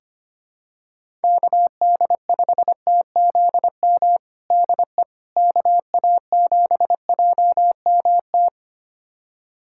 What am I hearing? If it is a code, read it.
Morse code, heard as a keyed tone, about 25 wpm, 700 Hz. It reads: KD5TZM DE KA7JMT
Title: USA Callsign DE Callsign 25wpm